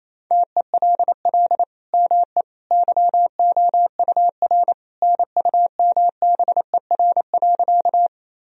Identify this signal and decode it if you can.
Morse code, heard as a keyed tone, about 28 wpm, 700 Hz. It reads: TELL ME YOUR NUMBER.
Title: Sentences from Top 1000 Words - Review Part 1 - 28wpm